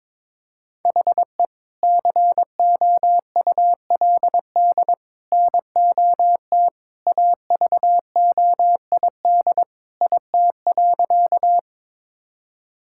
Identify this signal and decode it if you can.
Morse code, heard as a keyed tone, about 23 wpm, 700 Hz. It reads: HE COULD NOT AVOID IT.
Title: Sentences from Top 1000 Words - Review Part 6 - 22wpm